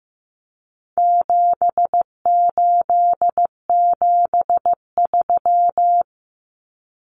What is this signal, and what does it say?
Morse code, heard as a keyed tone, about 15 wpm, 700 Hz. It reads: 7873
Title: Common Cut Numbers - 4-Digit 15wpm